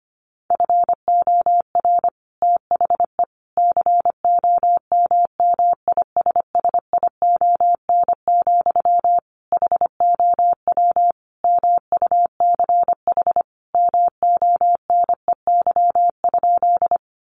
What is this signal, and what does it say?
Morse code, heard as a keyed tone, about 25 wpm, 700 Hz. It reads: FOR THE COMMISSION, HOW MUCH MONEY?
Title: Sentences from Top 1400 Words 25wpm